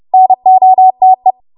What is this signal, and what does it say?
Morse code, heard as a keyed tone, about 30 wpm, 750 Hz. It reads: NOTE
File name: Morse / 30WPM